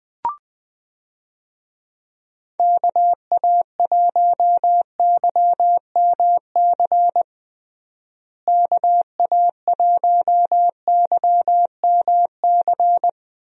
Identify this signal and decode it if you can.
Morse code, heard as a keyed tone, about 21 wpm, 700 Hz. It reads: KA1YMC KA1YMC
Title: USA Call Signs - 6 Characters - Repeated 3x - 20wpm